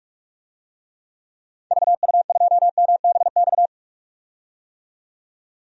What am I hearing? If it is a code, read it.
Morse code, heard as a keyed tone, about 47 wpm, 700 Hz. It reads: VU2MBX
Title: DXCC-100 Call Signs - 6 Characters - 45wpm